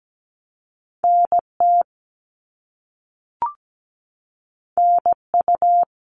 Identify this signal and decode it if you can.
Morse code, heard as a keyed tone, about 17 wpm, 700 Hz. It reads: NT NU
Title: Canadian Provinces and Territories - Repeated 3x - 17wpm